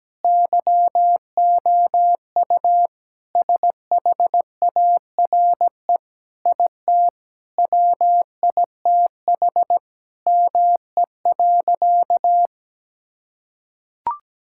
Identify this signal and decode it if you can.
Morse code, heard as a keyed tone, about 17 wpm, 700 Hz. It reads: YOU SHARE IT WITH ME.
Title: Sentences from Top 1000 Words - Review Part 6 - 17wpm